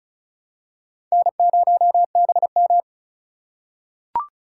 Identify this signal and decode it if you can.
Morse code, heard as a keyed tone, about 36 wpm, 700 Hz. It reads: N0BM
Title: USA Call Signs - 4 Characters - 2x Character Spacing - 35wpm